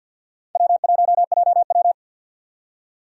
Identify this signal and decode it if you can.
Morse code, heard as a keyed tone, about 50 wpm, 700 Hz. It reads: W1JW
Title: USA Call Signs - 4 Characters - 50wpm